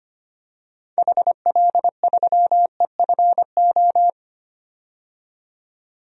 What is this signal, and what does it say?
Morse code, heard as a keyed tone, about 25 wpm, 700 Hz. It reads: HL3EFO
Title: DXCC-100 Call Signs - 6 Characters - 25wpm